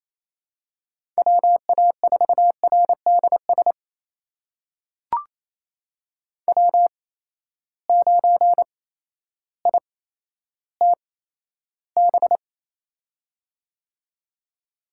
Morse code, heard as a keyed tone, about 28 wpm, spaced as for 13 wpm, 700 Hz. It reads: WA4RDS W9ITB
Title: USA Call Signs - 8x Character Spacing - 28wpm